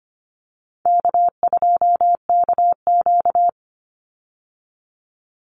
Morse code, heard as a keyed tone, about 25 wpm, 700 Hz. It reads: K2KQ
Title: Call Signs Repeated 3x - 25wpm